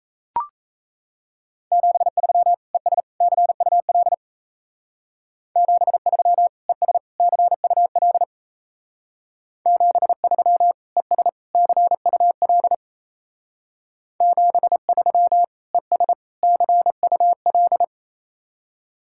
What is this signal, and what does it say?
Morse code, heard as a keyed tone, about 35 wpm, 700 Hz. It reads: 73 ES CUL 73 ES CUL 73 ES CUL 73 ES CUL